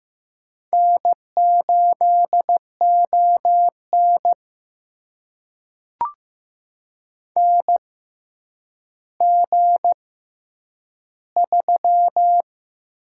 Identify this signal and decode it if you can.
Morse code, heard as a keyed tone, about 15 wpm, spaced as for 9 wpm, 700 Hz. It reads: N8ON NG3
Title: USA Call Signs - 6x Character Spacing - 15wpm